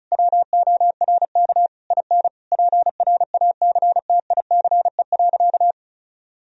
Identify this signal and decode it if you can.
Morse code, heard as a keyed tone, about 34 wpm, 700 Hz. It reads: WORK IN PRACTICE.
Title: Sentences from Top 1000 Words - Review Part 4 - 35wpm